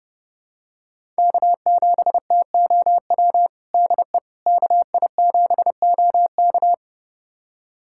Morse code, heard as a keyed tone, about 30 wpm, 700 Hz. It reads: K7TOW DE KI7OK
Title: USA Callsign DE Callsign - 5 Characters - 30wpm